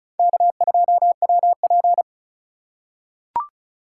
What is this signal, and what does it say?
Morse code, heard as a keyed tone, about 36 wpm, 700 Hz. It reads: K2WP
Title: USA Call Signs - 4 Characters - 4x Character Spacing - 35wpm